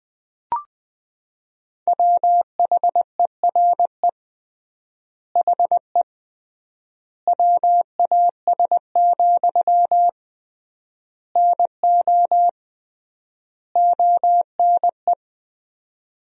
Morse code, heard as a keyed tone, about 21 wpm, 700 Hz. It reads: WHERE HE WAS, NO ONE
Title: 3x Word-Spacing - Sentences from Top 400 Words - 20wpm